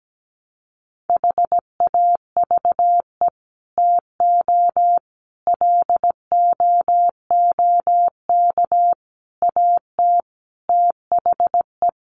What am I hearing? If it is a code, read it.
Morse code, heard as a keyed tone, about 17 wpm, 700 Hz. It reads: HAVE TO LOOK AT THE
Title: Sets of 5 Words Encore 17wpm